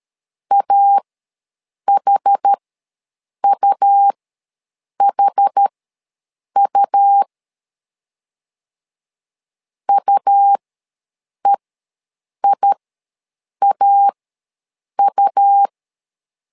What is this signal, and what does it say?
Morse code, heard as a keyed tone, about 13 wpm, spaced as for 7 wpm, 800 Hz. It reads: AHUHU UEIAU
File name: LEZIONI / 2 UHDN / 2 UH ws 12wpm 1730